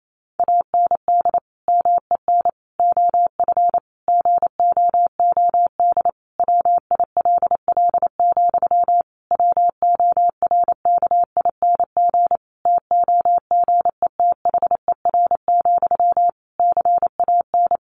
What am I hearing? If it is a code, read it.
Morse code, heard as a keyed tone, about 28 wpm, 700 Hz. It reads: AND MEN OF GOOD WILL, WORKING TOGETHER, CAN
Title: Sentences from Top 1400 Words 28wpm